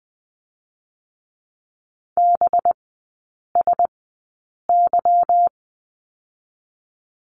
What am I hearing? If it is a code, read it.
Morse code, heard as a keyed tone, about 20 wpm, spaced as for 8 wpm, 700 Hz. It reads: BSY
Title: 184 Lesson 30e - Letter b - Three Characters 20wpm